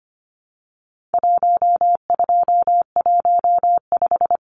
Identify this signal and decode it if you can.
Morse code, heard as a keyed tone, about 25 wpm, 700 Hz. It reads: 1215
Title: Numbers - 4-Digit 25wpm